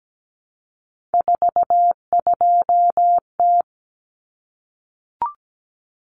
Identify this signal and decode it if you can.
Morse code, heard as a keyed tone, about 17 wpm, 690 Hz. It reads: 42T
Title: Common Cut Numbers - 3-Digit 17wpm